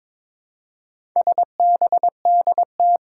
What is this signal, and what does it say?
Morse code, heard as a keyed tone, about 23 wpm, 700 Hz. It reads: SBDT
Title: Four Challenging Letter-Number Combinations 22wpm